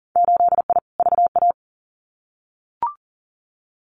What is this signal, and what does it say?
Morse code, heard as a keyed tone, about 40 wpm, 700 Hz. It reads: 8I VA
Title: Winter Field Day Exchanges Encore 40wpm